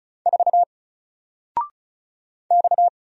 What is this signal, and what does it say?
Morse code, heard as a keyed tone, about 36 wpm, 700 Hz. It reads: 4 X
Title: Single Letter-Number Alt 35wpm